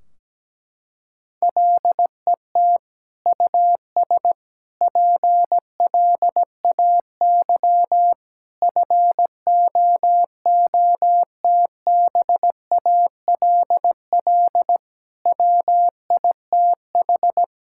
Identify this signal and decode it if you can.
Morse code, heard as a keyed tone, about 17 wpm, 700 Hz. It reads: LET US PLAY FOOTBALL WITH
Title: Sentences from Top 1600 Words 17wpm